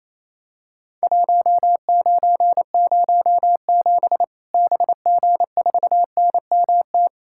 Morse code, heard as a keyed tone, about 28 wpm, 700 Hz. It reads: 1907 BG4NMT
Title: Worked All Europe DX Contest - QTC Exchange - 28wpm